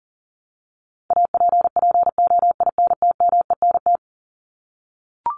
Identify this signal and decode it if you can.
Morse code, heard as a keyed tone, about 40 wpm, 700 Hz. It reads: APPOINTMENT
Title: Top 1751-2000 Words 40wpm